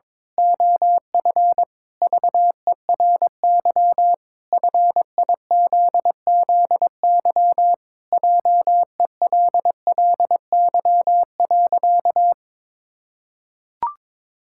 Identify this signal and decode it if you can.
Morse code, heard as a keyed tone, about 22 wpm, 700 Hz. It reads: OF VERY FIZZY JELLY.
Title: Pangram Challenge v2 - 22wpm